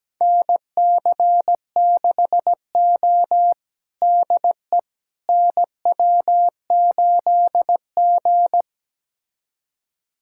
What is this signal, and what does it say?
Morse code, heard as a keyed tone, about 17 wpm, 700 Hz. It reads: NC6O DE NW8G
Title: USA Callsign DE Callsign - 4 Characters - 17wpm